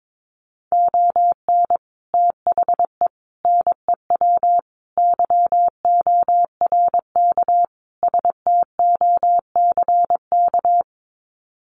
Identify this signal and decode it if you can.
Morse code, heard as a keyed tone, about 22 wpm, 700 Hz. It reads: ON THE NEW YORK STOCK
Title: Sets of 5 Words Encore 22wpm